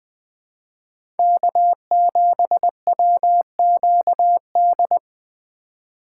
Morse code, heard as a keyed tone, about 21 wpm, 700 Hz. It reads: K7WQD
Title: USA Call Signs - 5 Characters - 2x Character Spacing - 20wpm